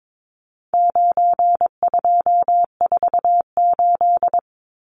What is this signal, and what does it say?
Morse code, heard as a keyed tone, about 22 wpm, 700 Hz. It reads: 9248
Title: Numbers - 4-Digit 22wpm